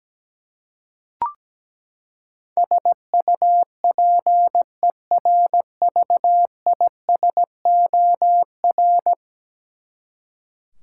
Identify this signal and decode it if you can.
Morse code, heard as a keyed tone, about 17 wpm, 700 Hz. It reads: SUPERVISOR
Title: Practice QSO - Exchange 3 - More Info - 17wpm